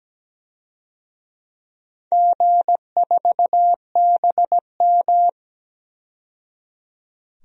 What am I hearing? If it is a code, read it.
Morse code, heard as a keyed tone, about 17 wpm, 700 Hz. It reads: G4BM
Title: DXCC-100 Call Signs - 4 Characters - 17wpm